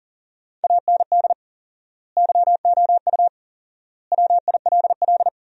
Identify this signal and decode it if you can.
Morse code, heard as a keyed tone, about 37 wpm, 700 Hz. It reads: AND YOU WILL
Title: 4x Word-Spacing - Sentences from Top 400 Words - 40wpm